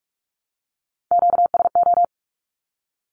Morse code, heard as a keyed tone, about 44 wpm, 700 Hz. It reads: QSO
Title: Q-Codes Repeated 3x 45wpm